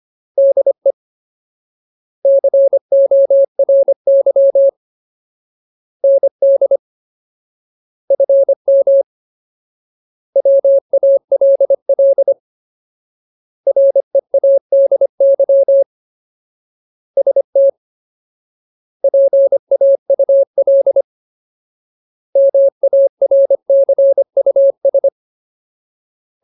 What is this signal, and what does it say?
Morse code, heard as a keyed tone, about 25 wpm, 550 Hz. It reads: DE CORY ND FM WALL READY ST PAUL MARCUS
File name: Ultimate Word List / 25WPM Ultimate